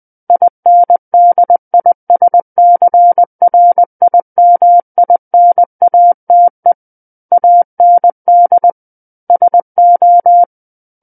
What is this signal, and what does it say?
Morse code, heard as a keyed tone, about 20 wpm, 700 Hz. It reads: INDISCRIMINATE AND SO
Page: SKCC Learning Center - Morse Code Practice Files